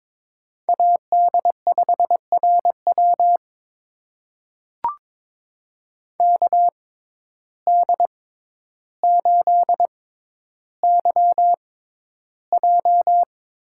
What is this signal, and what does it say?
Morse code, heard as a keyed tone, about 22 wpm, spaced as for 11 wpm, 700 Hz. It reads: AD5RW KD8YJ
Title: USA Call Signs - 6x Character Spacing - 22wpm